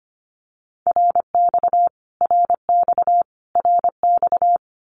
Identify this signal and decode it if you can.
Morse code, heard as a keyed tone, about 25 wpm, 700 Hz. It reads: RX RX RX
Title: Abbreviations Repeated 3x - 25wpm